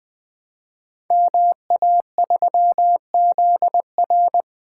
Morse code, heard as a keyed tone, about 20 wpm, 700 Hz. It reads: MA3ZR
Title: Code Groups - 5 Letter:Numbers - 20wpm